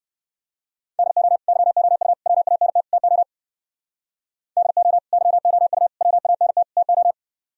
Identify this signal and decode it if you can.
Morse code, heard as a keyed tone, about 58 wpm, 700 Hz. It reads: DO YOU WANT TO DO YOU WANT TO